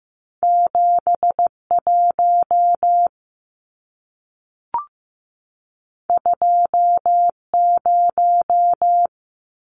Morse code, heard as a keyed tone, about 15 wpm, 700 Hz. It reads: 71 20
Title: Numbers - 2-Digit 15wpm